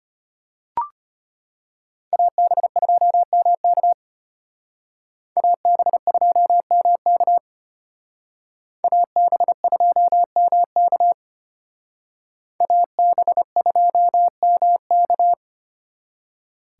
Morse code, heard as a keyed tone, about 31 wpm, 700 Hz. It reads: AB2MK AB2MK AB2MK AB2MK